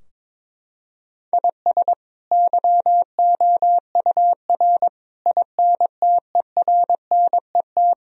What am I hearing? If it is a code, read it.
Morse code, heard as a keyed tone, about 22 wpm, 700 Hz. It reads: IS YOUR INTERNET
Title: Sentences from Top 1000 Words - Review Part 4 - 22wpm